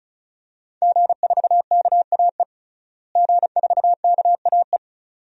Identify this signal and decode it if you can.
Morse code, heard as a keyed tone, about 36 wpm, 700 Hz. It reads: G4KAE G4KAE
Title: DXCC-100 Call Signs - 5 Characters - Repeated 2x - 35wpm